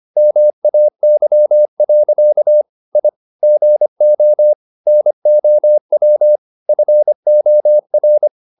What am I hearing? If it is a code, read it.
Morse code, heard as a keyed tone, about 25 wpm, 600 Hz. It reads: MAY. I GO NOW FOR
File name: the old man and the sea part 1 25wpm